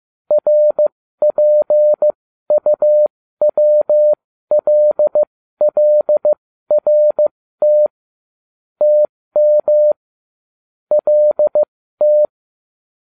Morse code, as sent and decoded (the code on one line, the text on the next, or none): .-. .--. ..- .-- .-.. .-.. .-. - / - -- / .-.. -
RPUWLLRT TM LT